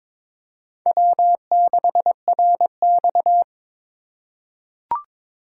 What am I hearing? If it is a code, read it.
Morse code, heard as a keyed tone, about 21 wpm, 700 Hz. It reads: W6RX
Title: USA Call Signs - 4 Characters - 3x Character Spacing - 22wpm